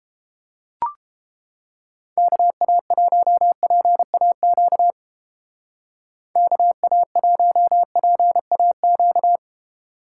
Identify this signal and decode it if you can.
Morse code, heard as a keyed tone, about 29 wpm, 700 Hz. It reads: KA1PAQ KA1PAQ